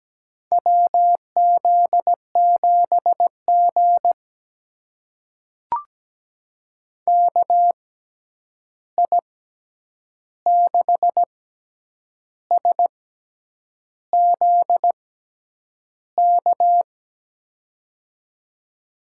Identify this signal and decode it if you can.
Morse code, heard as a keyed tone, about 17 wpm, spaced as for 8 wpm, 700 Hz. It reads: WZ7G KI6SZK
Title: USA Call Signs - 6x Character Spacing - 17wpm